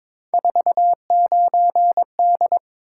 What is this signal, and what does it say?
Morse code, heard as a keyed tone, about 22 wpm, 700 Hz. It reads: 49D
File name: practice / 22wpm / Three Letter-Number Combinations - 22wpm